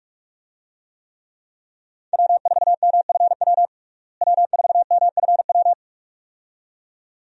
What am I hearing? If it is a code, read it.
Morse code, heard as a keyed tone, about 46 wpm, 700 Hz. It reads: W4MFW W4MFW
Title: USA Call Signs - 5 Characters - Repeated 2x - 45wpm